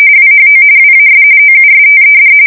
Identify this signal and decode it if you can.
RTTY, about 45.45 baud. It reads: WHO WAS OUSTED
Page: 170 Shift 2125/2295